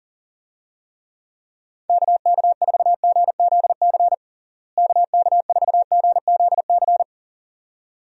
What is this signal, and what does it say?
Morse code, heard as a keyed tone, about 40 wpm, 700 Hz. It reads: KK4GZC KK4GZC
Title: USA Call Signs - 6 Characters - Repeated 2x - 40wpm